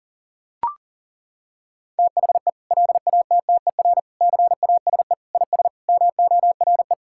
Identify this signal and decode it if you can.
Morse code, heard as a keyed tone, about 37 wpm, 700 Hz. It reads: THE LATTER CASE IS MORE
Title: Sentences from Top 1600 Words 40wpm